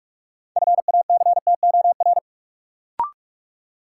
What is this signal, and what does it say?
Morse code, heard as a keyed tone, about 46 wpm, 700 Hz. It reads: FACTOR
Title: Sentences from Top 1300 Words - New Word Drill Alt - 45wpm